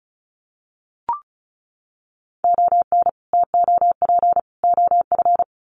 Morse code, heard as a keyed tone, about 34 wpm, 700 Hz. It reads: ON TOP OF
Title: Sets of 3 Words 35wpm